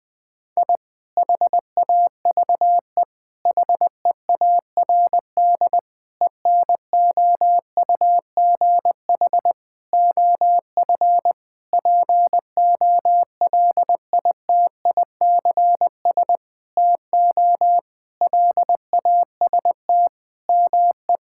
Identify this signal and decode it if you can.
Morse code, heard as a keyed tone, about 20 wpm, 700 Hz. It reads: I HAVE HEARD ENOUGH OF POLITICS TO LAST ME
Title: Sentences from Top 1000 Words - Review Part 6 - 20wpm